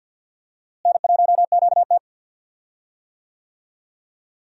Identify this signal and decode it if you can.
Morse code, heard as a keyed tone, about 47 wpm, 700 Hz. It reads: N1QT
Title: USA Call Signs - 4 Characters - Repeated 2x - 50wpm